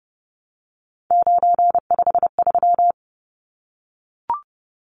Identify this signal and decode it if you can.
Morse code, heard as a keyed tone, about 30 wpm, 700 Hz. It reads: 953
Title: Numbers - 3-Digit 30wpm